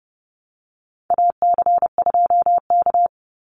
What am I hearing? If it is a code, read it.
Morse code, heard as a keyed tone, about 30 wpm, 700 Hz. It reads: AC2K
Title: Call Signs Repeated 2x - 30wpm